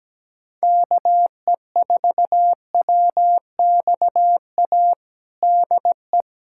Morse code, heard as a keyed tone, about 17 wpm, 700 Hz. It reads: KE4WXA DE
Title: USA Callsign DE Callsign - 6 Characters - 17wpm